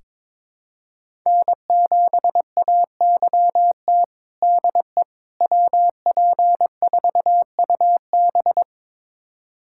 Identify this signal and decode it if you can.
Morse code, heard as a keyed tone, about 22 wpm, 700 Hz. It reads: N7AYT DE WP4UB
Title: USA Callsign DE Callsign - 5 Characters - 22wpm